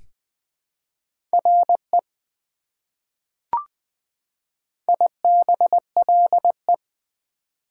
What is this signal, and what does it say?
Morse code, heard as a keyed tone, about 20 wpm, 700 Hz. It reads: RE IBLE
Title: Prefixes and Suffixes 20wpm